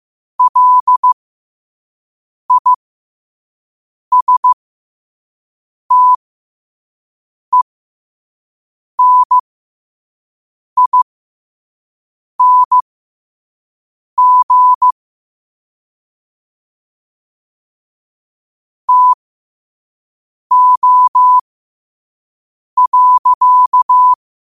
Morse code, heard as a keyed tone, about 15 wpm, spaced as for 5 wpm, 1000 Hz. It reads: LISTENING TO.
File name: qotdfiles / 5wpm